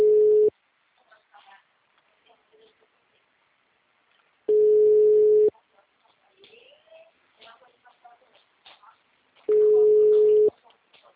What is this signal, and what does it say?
Ringback tone (Europe, 425 Hz).
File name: out- -2015- - - .69347